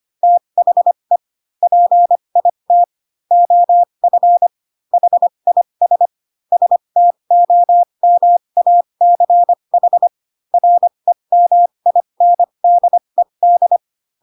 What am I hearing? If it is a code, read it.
Morse code, heard as a keyed tone, about 25 wpm, spaced as for 22 wpm, 700 Hz. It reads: THE PIT OF HIS STOMACH REMINDED